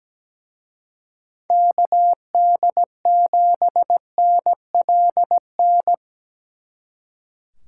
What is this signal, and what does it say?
Morse code, heard as a keyed tone, about 20 wpm, 700 Hz. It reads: KD7NLN
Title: Speed Racing - USA 2x3 Call Signs 17wpm